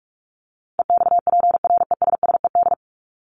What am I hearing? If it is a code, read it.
Morse code, heard as a keyed tone, about 44 wpm, 700 Hz. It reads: EXPRESSED
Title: ED Suffix - Top 100 Words with Base Words 45wpm